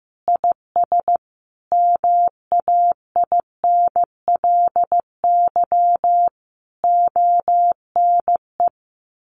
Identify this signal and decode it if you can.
Morse code, heard as a keyed tone, about 14 wpm, 700 Hz. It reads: IS MAINLY ONE
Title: Sentences from Top 1400 Words 15wpm